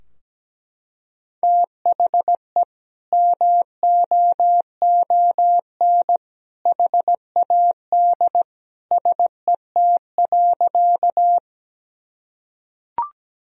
Morse code, heard as a keyed tone, about 17 wpm, 700 Hz. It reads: THE MOON HAD SET.
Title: Sentences from Top 1000 Words - Review Part 3 - 17wpm